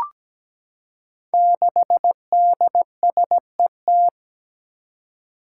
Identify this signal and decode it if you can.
Morse code, heard as a keyed tone, about 17 wpm, 700 Hz. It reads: 6DSET
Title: Five Challenging Letter-Number Combinations 17wpm